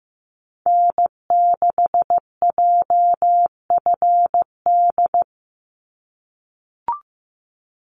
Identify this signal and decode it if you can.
Morse code, heard as a keyed tone, about 15 wpm, 700 Hz. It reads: N6JFD
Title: NA SOTA Activator Callsigns 15wpm